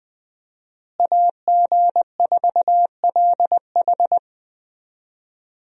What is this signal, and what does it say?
Morse code, heard as a keyed tone, about 20 wpm, 700 Hz. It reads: AG4LH
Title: USA Call Signs - 5 Characters - Repeated 3x - 20wpm